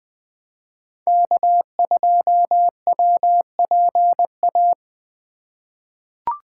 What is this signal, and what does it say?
Morse code, heard as a keyed tone, about 20 wpm, 700 Hz. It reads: K2WPA
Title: 2023 POTA Activator Call Signs - 20wpm